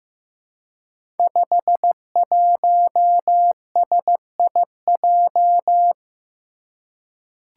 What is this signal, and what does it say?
Morse code, heard as a keyed tone, about 15 wpm, 700 Hz. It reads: 51SIJ
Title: Five Challenging Letter-Number Combinations 15wpm